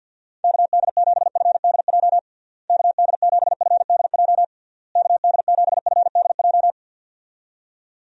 Morse code, heard as a keyed tone, about 52 wpm, 700 Hz. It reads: KD7FDJ KD7FDJ KD7FDJ
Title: USA Call Signs - 6 Characters - Repeated 3x - 50wpm